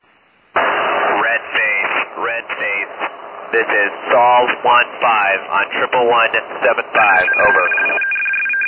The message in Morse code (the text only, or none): none